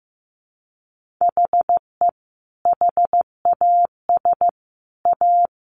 Morse code, heard as a keyed tone, about 14 wpm, 700 Hz. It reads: HE HAS A
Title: Sentences from Top 1400 Words 15wpm